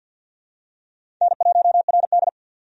Morse code, heard as a keyed tone, about 51 wpm, 700 Hz. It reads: N1RD
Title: USA Call Signs - 4 Characters - 50wpm